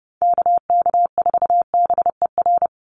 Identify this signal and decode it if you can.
Morse code, heard as a keyed tone, about 30 wpm, 700 Hz. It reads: KK4BER
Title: USA Call Signs 30wpm